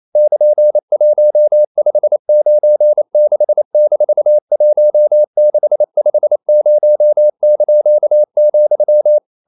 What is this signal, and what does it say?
Morse code, heard as a keyed tone, about 28 wpm, 600 Hz. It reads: <KN>1596-1650),
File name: CW / Practice / EnglishQuotations0128